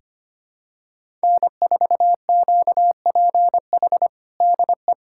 Morse code, heard as a keyed tone, about 25 wpm, 700 Hz. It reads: N4QPH DE
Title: USA Callsign DE Callsign 25wpm